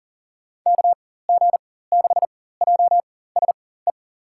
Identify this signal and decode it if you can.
Morse code, heard as a keyed tone, about 40 wpm, spaced as for 19 wpm, 700 Hz. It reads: KG6JSE
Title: USA Call Signs - 6 Characters - 4x Character Spacing - 40wpm